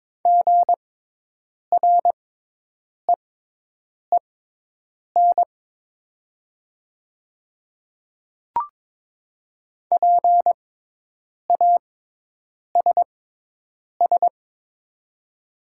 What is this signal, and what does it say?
Morse code, heard as a keyed tone, about 22 wpm, spaced as for 8 wpm, 700 Hz. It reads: GREEN PASS
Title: 6x Character Spacing - Top 500 Words - 22wpm